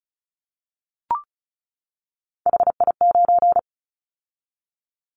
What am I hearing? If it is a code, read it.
Morse code, heard as a keyed tone, about 36 wpm, 700 Hz. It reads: HI9
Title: Callsign Prefixes for Top 100 DXCC - 35wpm